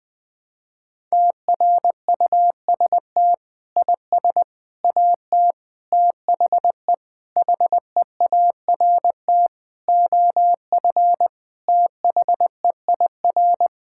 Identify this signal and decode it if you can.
Morse code, heard as a keyed tone, about 20 wpm, 700 Hz. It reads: TRUST IS AT THE HEART OF THEIR
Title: Sentences with 5-Word Phrases - 20wpm